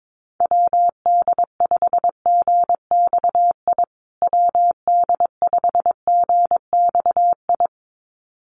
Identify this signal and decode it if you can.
Morse code, heard as a keyed tone, about 22 wpm, 700 Hz. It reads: WD5GXI WD5GXI
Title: USA Call Signs Repeated 2x 22wpm